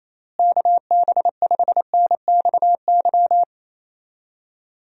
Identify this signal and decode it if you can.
Morse code, heard as a keyed tone, about 28 wpm, 700 Hz. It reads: KB5NXY
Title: USA Call Signs - 6 Characters - Repeated 3x - 28wpm